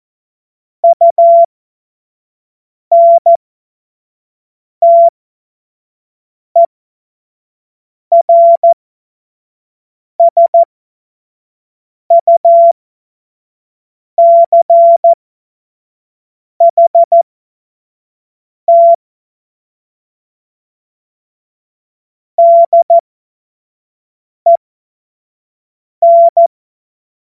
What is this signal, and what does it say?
Morse code, heard as a keyed tone, about 14 wpm, spaced as for 5 wpm, 680 Hz. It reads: UNTERSUCHT DEN
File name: Morsekurs / Witze / Witz 02 April 2024 25 F70